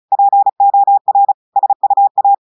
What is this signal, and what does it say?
Morse code, heard as a keyed tone, about 36 wpm, 800 Hz. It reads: POR SUA
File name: audios / portugais / portugais texte en clair 1 vitesse 35 mots minutes